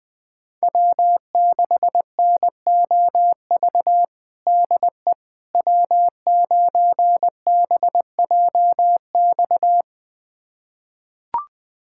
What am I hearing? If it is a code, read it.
Morse code, heard as a keyed tone, about 20 wpm, 700 Hz. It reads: W6NOV DE W9BJX
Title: USA Callsign DE Callsign - 5 Characters - 20wpm